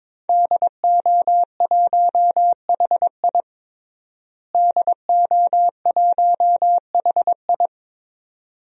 Morse code, heard as a keyed tone, about 22 wpm, 700 Hz. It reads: DO1HI DO1HI
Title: DXCC-100 Call Signs - 5 Characters - Repeated 2x - 22wpm